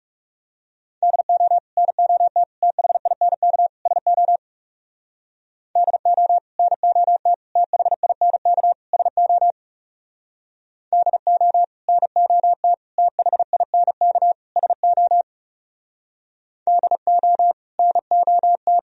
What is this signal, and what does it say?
Morse code, heard as a keyed tone, about 36 wpm, 700 Hz. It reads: DO NOT THINK SO DO NOT THINK SO DO NOT THINK SO DO NOT
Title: Speed-Racing - Sets of 4 Words 30wpm